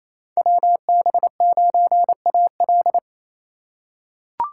Morse code, heard as a keyed tone, about 28 wpm, 700 Hz. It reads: WB9AL
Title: USA Call Signs - 5 Characters - 4x Character Spacing - 28wpm